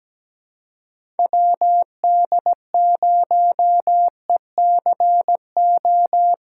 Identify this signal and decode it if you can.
Morse code, heard as a keyed tone, about 17 wpm, 700 Hz. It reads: WD0ECO
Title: K1USN SST QSO Practice - 17wpm